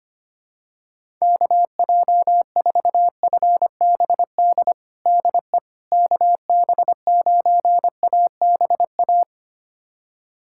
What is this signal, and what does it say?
Morse code, heard as a keyed tone, about 25 wpm, 700 Hz. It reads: KJ4FBD DE KB9ABA
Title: USA Callsign DE Callsign - 6 Characters - 25wpm